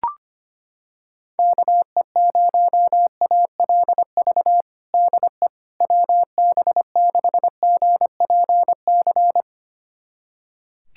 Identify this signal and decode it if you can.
Morse code, heard as a keyed tone, about 25 wpm, 700 Hz. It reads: KE0ALV DE WB6GPC
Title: USA Callsign DE Callsign - 6 Characters - 25wpm